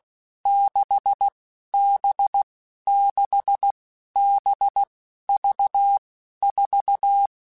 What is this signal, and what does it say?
Morse code, heard as a keyed tone, about 16 wpm, 800 Hz. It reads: 6 B 6 B V 4
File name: Basic Letter Groups By7Groups / VB46 / VB46@15